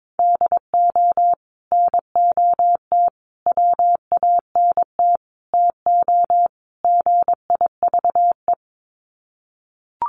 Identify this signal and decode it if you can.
Morse code, heard as a keyed tone, about 22 wpm, 700 Hz. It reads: DO NOT WANT TO GIVE
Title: Sets of 5 Words Encore 22wpm